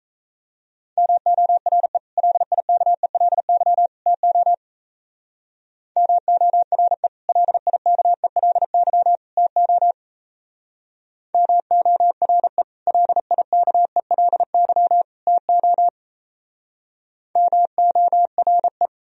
Morse code, heard as a keyed tone, about 34 wpm, 700 Hz. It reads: MORE LIKELY TO MORE LIKELY TO MORE LIKELY TO MORE
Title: Speed-Racing - Sets of 3 Words 28wpm